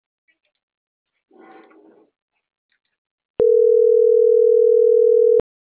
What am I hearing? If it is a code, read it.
Ringback tone (North America), 440 + 480 Hz.